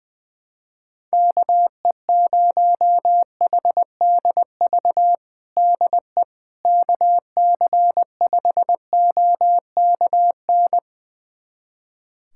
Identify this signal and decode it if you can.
Morse code, heard as a keyed tone, about 20 wpm, 700 Hz. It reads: KE0HDV DE KC5OKN
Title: USA Callsign DE Callsign 20wpm